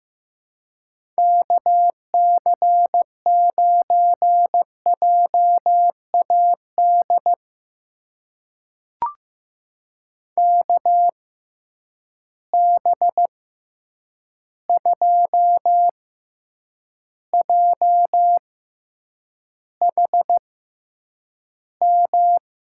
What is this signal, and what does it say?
Morse code, heard as a keyed tone, about 15 wpm, spaced as for 8 wpm, 700 Hz. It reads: KC9JAD KB2JHM
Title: USA Call Signs - 6 Characters - 6x Character Spacing - 15wpm